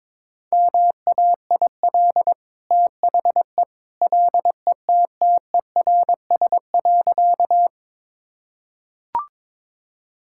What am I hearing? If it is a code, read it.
Morse code, heard as a keyed tone, about 22 wpm, 690 Hz. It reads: MAIL THE LETTERS.
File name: practice / 22wpm / Easy Sentences with Hints - 22wpm